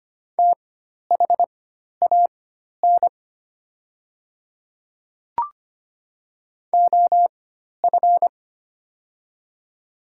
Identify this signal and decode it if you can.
Morse code, heard as a keyed tone, about 25 wpm, spaced as for 12 wpm, 700 Hz. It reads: THAN OF
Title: 4x Character Spacing - Top 100 Words - 25wpm